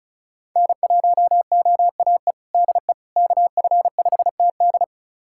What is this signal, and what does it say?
Morse code, heard as a keyed tone, about 34 wpm, 700 Hz. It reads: N1OAE DE KF5TD
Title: USA Callsign DE Callsign - 5 Characters - 35wpm